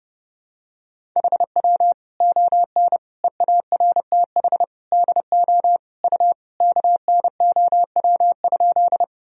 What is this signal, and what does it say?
Morse code, heard as a keyed tone, about 30 wpm, 700 Hz. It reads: HW ON EARTH DO U KNOW?
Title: Sentences from Top 200 Words and Abbreviations 30wpm